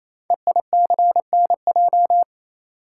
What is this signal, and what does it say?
Morse code, heard as a keyed tone, about 29 wpm, 700 Hz. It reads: EICNJ
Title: Code Groups - 5 Letters - 28wpm